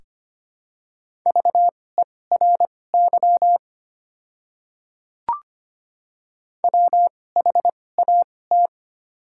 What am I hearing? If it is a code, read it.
Morse code, heard as a keyed tone, about 25 wpm, spaced as for 18 wpm, 700 Hz. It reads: VERY WHAT
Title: 2x Character Spacing - Top 200 Words - 25wpm